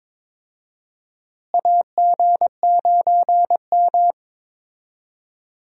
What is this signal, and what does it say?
Morse code, heard as a keyed tone, about 22 wpm, 700 Hz. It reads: AG9M
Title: USA Call Signs - 4 Characters - Repeated 2x - 22wpm